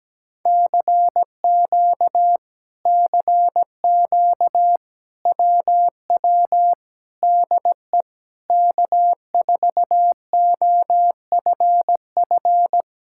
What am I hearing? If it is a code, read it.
Morse code, heard as a keyed tone, about 17 wpm, 700 Hz. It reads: CQ CQ WW DE K4OFF
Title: Speed-Racing Common QSO Phrases - 17wpm